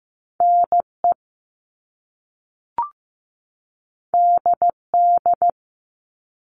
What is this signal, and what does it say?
Morse code, heard as a keyed tone, about 15 wpm, 700 Hz. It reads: NE DD
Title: All Cut Numbers - 2-Digit 15wpm